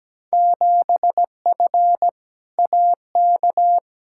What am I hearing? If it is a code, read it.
Morse code, heard as a keyed tone, about 17 wpm, 700 Hz. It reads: 7F AK
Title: Field Day Exchanges Encore - 17wpm